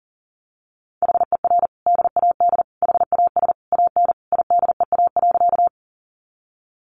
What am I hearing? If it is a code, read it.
Morse code, heard as a keyed tone, about 40 wpm, 700 Hz. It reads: HER DAD HAS AN IDEA.
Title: Additional Easy Phrases 40wpm